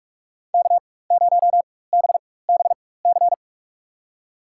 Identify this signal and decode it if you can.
Morse code, heard as a keyed tone, about 44 wpm, spaced as for 21 wpm, 700 Hz. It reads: K0BBC
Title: USA Call Signs - 5 Characters - 4x Character Spacing - 45wpm